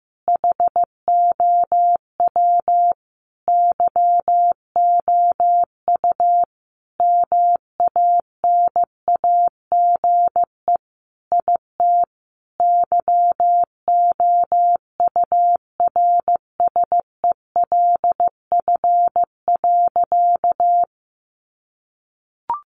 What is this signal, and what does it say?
Morse code, heard as a keyed tone, about 15 wpm, 700 Hz. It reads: HOW YOU MANAGE IT YOURSELF.
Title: Sentences from Top 1400 Words 15wpm